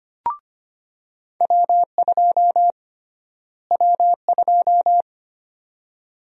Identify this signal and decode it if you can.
Morse code, heard as a keyed tone, about 25 wpm, 700 Hz. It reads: W2 W2
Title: USA Call Sign Prefixes - Repeated 2x - 25wpm